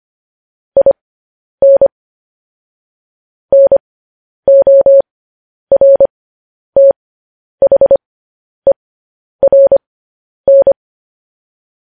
Morse code, heard as a keyed tone, about 25 wpm, spaced as for 10 wpm, 550 Hz. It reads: IN NORTHERN